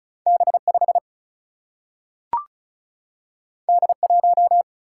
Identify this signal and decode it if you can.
Morse code, heard as a keyed tone, about 36 wpm, 700 Hz. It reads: B 5 D 1
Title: Single-Letter Single-Number 35wpm